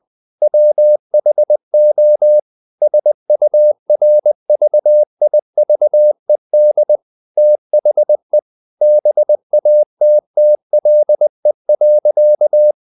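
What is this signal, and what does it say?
Morse code, heard as a keyed tone, about 20 wpm, 600 Hz. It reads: WHO SURVIVED THE BATTLE.
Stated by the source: Morse code at 19 wpm from CW med Gnister, reading Ken Follett's novel Never in English.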